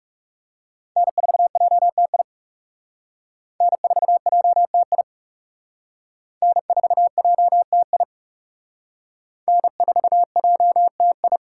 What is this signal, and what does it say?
Morse code, heard as a keyed tone, about 35 wpm, 700 Hz. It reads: N4JTI N4JTI N4JTI N4JTI
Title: Speed Racing - USA 1x3 Call Signs 30wpm